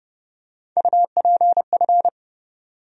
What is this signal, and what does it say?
Morse code, heard as a keyed tone, about 30 wpm, 700 Hz. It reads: UPF
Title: USA Call Sign Postfixes - 3 Letters 30wpm